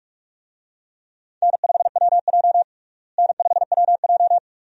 Morse code, heard as a keyed tone, about 44 wpm, 700 Hz. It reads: N5WJ N5WJ
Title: USA Call Signs - 4 Characters - Repeated 2x - 45wpm